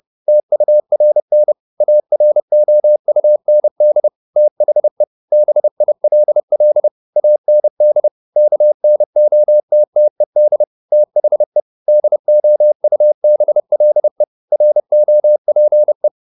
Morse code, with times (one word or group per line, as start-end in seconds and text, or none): 0.3-1.5 TURN
1.8-4.1 AROUND
4.4-5.0 THE
5.3-6.9 BILL
7.2-8.1 AND
8.4-10.6 KNOTTED
10.9-11.6 THE
11.9-14.2 DOUBLE
14.5-16.1 ROPE